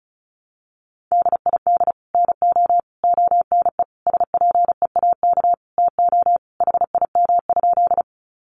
Morse code, heard as a keyed tone, about 34 wpm, 700 Hz. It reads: DID NO ONE SPEAK TO HIM?
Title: Sentences from Top 1100 Words 35wpm